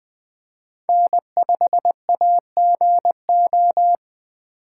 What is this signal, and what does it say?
Morse code, heard as a keyed tone, about 21 wpm, 700 Hz. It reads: N5AGO